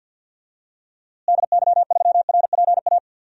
Morse code, heard as a keyed tone, about 53 wpm, 700 Hz. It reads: DY3RPA
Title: DXCC-100 Call Signs - 6 Characters - Repeated 2x - 50wpm